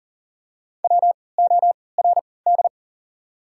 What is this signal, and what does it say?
Morse code, heard as a keyed tone, about 40 wpm, spaced as for 23 wpm, 690 Hz. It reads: WORD
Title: 3x Character Spacing - Top 100 Words - 40wpm